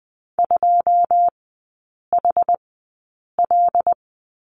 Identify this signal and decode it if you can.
Morse code, heard as a keyed tone, about 20 wpm, spaced as for 8 wpm, 700 Hz. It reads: 2HL
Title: 114 Lesson 19e - Letter w - Three Characters 20wpm